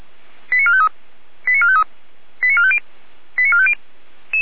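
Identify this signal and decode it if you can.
Five-tone selective call (ZVEI), heard as quick runs of five tones, tones of about 70 ms. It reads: 89453 89453 89450 89450
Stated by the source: SDR receiver recording of ZVEI five-tone selective call 88074, from feuerwehr-orlamuende.org (fire brigade Orlamünde).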